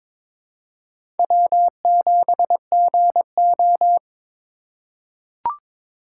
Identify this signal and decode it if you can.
Morse code, heard as a keyed tone, about 22 wpm, 700 Hz. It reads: W7GO
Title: USA Call Signs - 4 Characters - 2x Character Spacing - 22wpm